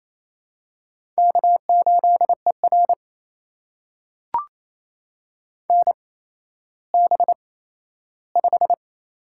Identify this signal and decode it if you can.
Morse code, heard as a keyed tone, about 28 wpm, spaced as for 14 wpm, 700 Hz. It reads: K8ER NB5
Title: USA Call Signs - 4 Characters - 8x Character Spacing - 28wpm